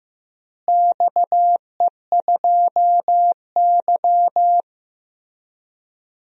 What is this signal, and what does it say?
Morse code, heard as a keyed tone, about 15 wpm, 700 Hz. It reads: XE2Y
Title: DXCC-100 Call Signs - 4 Characters - Repeated 2x - 15wpm